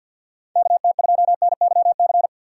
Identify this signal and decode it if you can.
Morse code, heard as a keyed tone, about 53 wpm, 700 Hz. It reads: KT2NYC
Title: USA Call Signs - 6 Characters - 8x Character Spacing - 50wpm